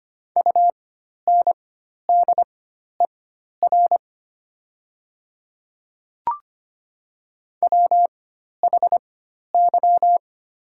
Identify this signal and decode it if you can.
Morse code, heard as a keyed tone, about 25 wpm, spaced as for 12 wpm, 690 Hz. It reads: UNDER WHY
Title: 4x Character Spacing - Top 200 Words - 25wpm